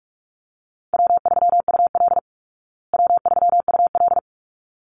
Morse code, heard as a keyed tone, about 44 wpm, 700 Hz. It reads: W3UL W3UL
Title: Easy Call Signs Repeated 2x - 45wpm